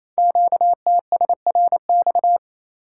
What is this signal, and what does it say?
Morse code, heard as a keyed tone, about 29 wpm, 700 Hz. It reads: QTSRX
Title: Code Groups - 5 Letters - 28wpm